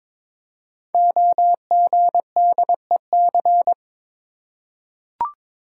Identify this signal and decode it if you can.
Morse code, heard as a keyed tone, about 21 wpm, 700 Hz. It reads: OGDEC
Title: Code Groups - 5 Letters - 22wpm